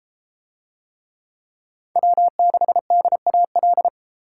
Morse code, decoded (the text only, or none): W6DAL